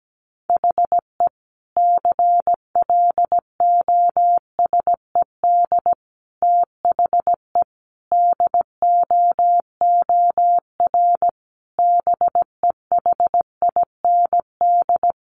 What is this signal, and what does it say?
Morse code, heard as a keyed tone, about 17 wpm, 700 Hz. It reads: HE CLOSED THE DOOR BEHIND
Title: Sentences from Top 1400 Words 17wpm